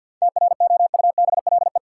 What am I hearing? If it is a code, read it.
Morse code, heard as a keyed tone, about 50 wpm, 700 Hz. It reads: TROUBLE
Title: Sentences from Top 1200 Words - New Word Drill Alt - 50wpm